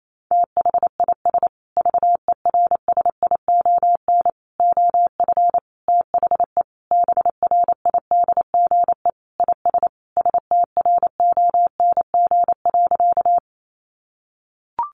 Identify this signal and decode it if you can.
Morse code, heard as a keyed tone, about 28 wpm, 700 Hz. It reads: THIS VERSION OF THE BRIDGE IS STRONG.
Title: Sentences from Top 1400 Words 28wpm